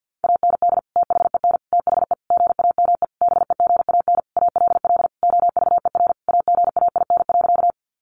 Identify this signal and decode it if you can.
Morse code, heard as a keyed tone, about 50 wpm, 700 Hz. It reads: AND THEN THE GAME BEGAN ALL OVER AGAIN.
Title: Sentences from Top 500 Words 50wpm